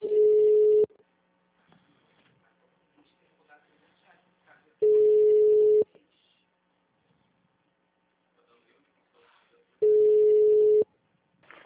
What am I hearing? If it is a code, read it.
Ringback tone (Europe, 425 Hz), with speech.